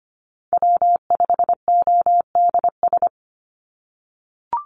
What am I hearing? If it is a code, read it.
Morse code, heard as a keyed tone, about 25 wpm, 700 Hz. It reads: W5ODS
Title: NA SOTA Activator Callsigns 25wpm